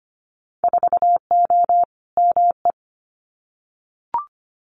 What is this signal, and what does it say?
Morse code, heard as a keyed tone, about 25 wpm, 700 Hz. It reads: 4O ME
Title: Winter Field Day Exchanges Encore 25wpm